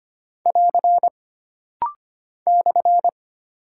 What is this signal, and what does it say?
Morse code, heard as a keyed tone, about 25 wpm, 700 Hz. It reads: <AR> /
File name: Single Punctuation-Prosign - Rapid-Fire - 25wpm